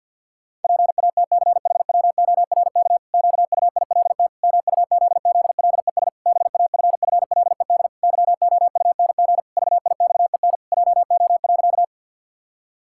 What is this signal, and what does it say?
Morse code, heard as a keyed tone, about 50 wpm, 700 Hz. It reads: PATCHWORK QUILT MUZZLES BAFFLED YOUNG VIXEN JO.
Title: Pangram Challenge v2 - 50wpm